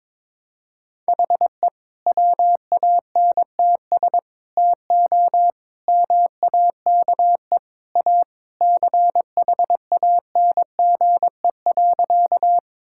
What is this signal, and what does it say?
Morse code, heard as a keyed tone, about 22 wpm, 700 Hz. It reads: HE WANTS TO MAKE A CHANGE.
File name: practice / 22wpm / Sentences with 3-Word Phrases - 22wpm